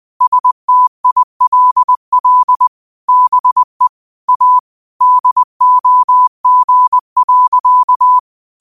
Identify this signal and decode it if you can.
Morse code, heard as a keyed tone, about 20 wpm, 1000 Hz. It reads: STILL BE A DOG.
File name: qotdfiles / 20wpm